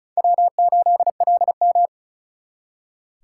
Morse code, heard as a keyed tone, about 45 wpm, 700 Hz. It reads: W8LM